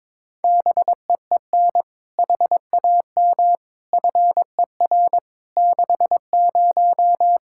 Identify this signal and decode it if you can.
Morse code, heard as a keyed tone, about 22 wpm, 700 Hz. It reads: BEEN HAM FER 60
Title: Practice QSO - Exchange 3 - More Info - 22wpm